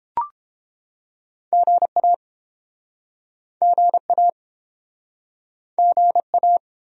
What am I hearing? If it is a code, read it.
Morse code, heard as a keyed tone, about 30 wpm, 700 Hz. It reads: GA GA GA